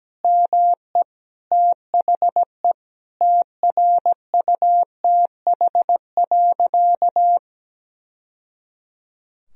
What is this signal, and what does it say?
Morse code, heard as a keyed tone, about 17 wpm, 700 Hz. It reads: ME THE TRUTH.
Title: Sentences from Top 1000 Words - Review Part 1 - 17wpm